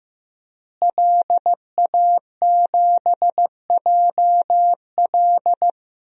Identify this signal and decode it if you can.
Morse code, heard as a keyed tone, about 15 wpm, 700 Hz. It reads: LA7JL
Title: DXCC-100 Call Signs - 5 Characters - Repeated 3x - 15wpm